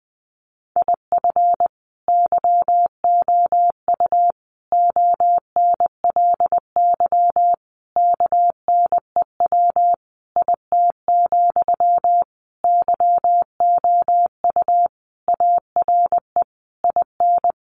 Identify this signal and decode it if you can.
Morse code, heard as a keyed tone, about 20 wpm, 700 Hz. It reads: IF YOU ONLY KNEW IT, YOU ARE IN
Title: Sentences from Top 1500 Words 20wpm